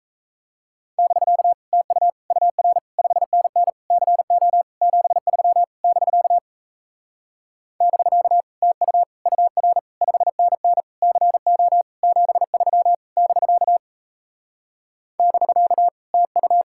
Morse code, heard as a keyed tone, about 37 wpm, 700 Hz. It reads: <BK> TU UR 5NN CO 73 <BK> <BK> TU UR 5NN CO 73 <BK> <BK> TU